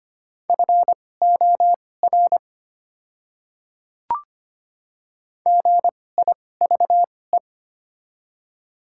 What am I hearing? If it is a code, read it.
Morse code, heard as a keyed tone, about 25 wpm, spaced as for 18 wpm, 700 Hz. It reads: FOR GIVE
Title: 2x Character Spacing - Top 300 Words - 25wpm